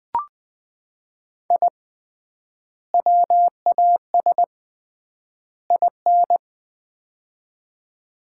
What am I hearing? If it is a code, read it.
Morse code, heard as a keyed tone, about 20 wpm, 700 Hz. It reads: I WAS IN
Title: 3x Word-Spacing - Sets of 3 Words - 20wpm